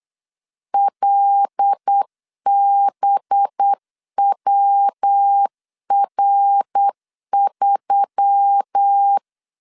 Morse code, heard as a keyed tone, about 8 wpm, 800 Hz. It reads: LBWR3
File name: LEZIONI / 6 PKX0 / 6 PK 8wpm 3003